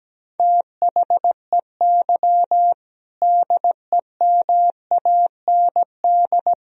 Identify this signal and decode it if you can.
Morse code, heard as a keyed tone, about 17 wpm, 700 Hz. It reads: THEY DEMAND
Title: Sentences from Top 1000 Words - Review Part 4 - 17wpm